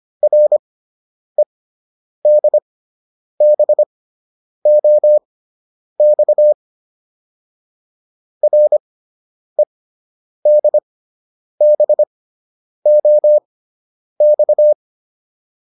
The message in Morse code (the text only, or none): REDBOX REDBOX